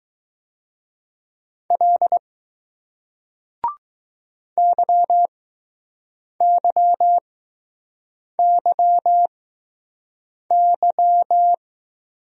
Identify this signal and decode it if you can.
Morse code, heard as a keyed tone, about 21 wpm, spaced as for 6 wpm, 700 Hz. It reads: L YYYY